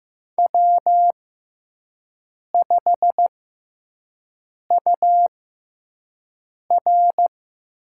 Morse code, heard as a keyed tone, about 15 wpm, spaced as for 5 wpm, 700 Hz. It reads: W5UR
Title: USA Call Signs - 5 Characters - 6x Character Spacing - 15wpm